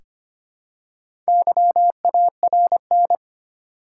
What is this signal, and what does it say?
Morse code, heard as a keyed tone, about 25 wpm, 700 Hz. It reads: YARN
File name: practice / 25wpm / Easy Words with Hints - 25wpm